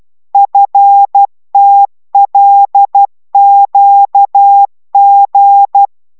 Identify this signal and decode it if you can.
Morse code, heard as a keyed tone, about 12 wpm, 800 Hz. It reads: FTLQG